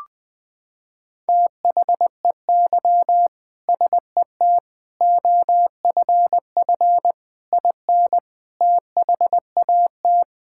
Morse code, heard as a keyed tone, about 20 wpm, 700 Hz. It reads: THEY SET OFF IN THAT
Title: Sentences from Top 1000 Words - Review Part 5 - 20wpm